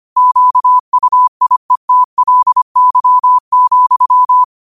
Morse code, heard as a keyed tone, about 25 wpm, 1000 Hz. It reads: QUIETLY,
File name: qotdfiles / 25wpm